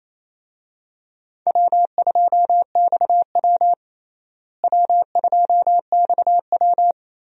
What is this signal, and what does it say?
Morse code, heard as a keyed tone, about 28 wpm, 700 Hz. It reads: W2XW W2XW
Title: USA Call Signs - 4 Characters - Repeated 3x - 28wpm